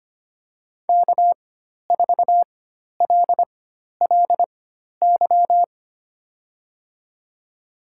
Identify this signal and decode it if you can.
Morse code, heard as a keyed tone, about 25 wpm, spaced as for 12 wpm, 700 Hz. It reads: K4LLY
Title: USA Call Signs - 5 Characters - 4x Character Spacing - 25wpm